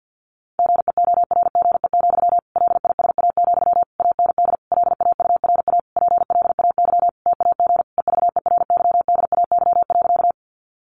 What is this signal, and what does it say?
Morse code, heard as a keyed tone, about 50 wpm, 700 Hz. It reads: GEORGE, LISA, AND LAURA PLAY TAG EVERYDAY.
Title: Additional Easy Phrases 50wpm